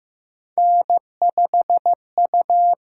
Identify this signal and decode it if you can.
Morse code, heard as a keyed tone, about 15 wpm, 700 Hz. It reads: N5U
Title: USA Call Signs - 5 Characters - 8x Character Spacing - 15wpm